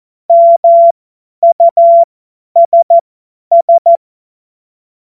Morse code, heard as a keyed tone, about 14 wpm, spaced as for 10 wpm, 680 Hz. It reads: MUSS
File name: Morsekurs / Witze / Witz 28. März 2023 50 F70